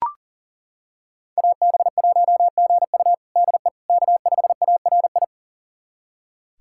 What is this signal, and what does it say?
Morse code, heard as a keyed tone, about 40 wpm, 700 Hz. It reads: AB1GU DE K5ARI
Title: USA Callsign DE Callsign - 5 Characters - 40wpm